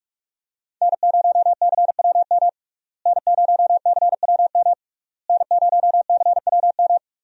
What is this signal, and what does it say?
Morse code, heard as a keyed tone, about 46 wpm, 700 Hz. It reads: N0CWM N0CWM N0CWM
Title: USA Call Signs - 5 Characters - Repeated 3x - 45wpm